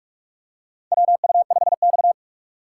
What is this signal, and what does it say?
Morse code, heard as a keyed tone, about 46 wpm, 700 Hz. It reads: WU5X
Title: USA Call Signs - 4 Characters - 6x Character Spacing - 45wpm